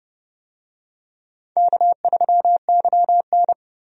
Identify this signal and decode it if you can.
Morse code, heard as a keyed tone, about 30 wpm, 700 Hz. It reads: K3YN
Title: USA Call Signs - 4 Characters - 30wpm